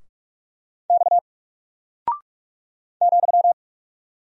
Morse code, heard as a keyed tone, about 44 wpm, 700 Hz. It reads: X ,
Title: Single Character (Letter:Number:Punctuation:Prosign) 45wpm